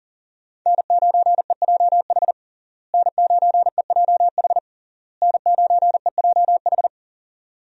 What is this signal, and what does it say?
Morse code, heard as a keyed tone, about 40 wpm, 700 Hz. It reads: N9EJH N9EJH N9EJH
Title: USA Call Signs - 5 Characters - Repeated 3x - 40wpm